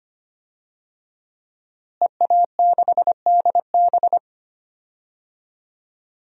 Morse code, heard as a keyed tone, about 25 wpm, 700 Hz. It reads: EA6DB
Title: DXCC-100 Call Signs - 5 Characters - Repeated 3x - 25wpm